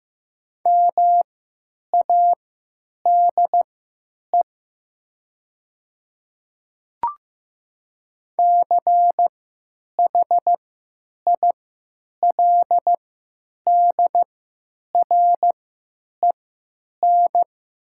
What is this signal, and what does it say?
Morse code, heard as a keyed tone, about 15 wpm, spaced as for 9 wpm, 700 Hz. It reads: MADE CHILDREN
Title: 3x Character Spacing - Top 500 Words - 15wpm